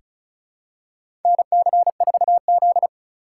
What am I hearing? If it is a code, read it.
Morse code, heard as a keyed tone, about 36 wpm, 690 Hz. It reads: NC4Z
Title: USA Call Signs - 4 Characters - 3x Character Spacing - 35wpm